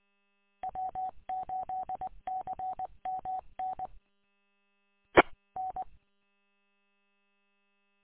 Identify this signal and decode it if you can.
Morse code, heard as a keyed tone, about 20 wpm, 750 Hz. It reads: W8CMN EN
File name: w8cmn- -combined